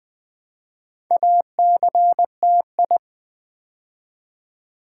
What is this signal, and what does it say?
Morse code, heard as a keyed tone, about 20 wpm, 700 Hz. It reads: ACTI
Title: Build A Word - 2 Letters At A Time - 8-Letter Words - 20wpm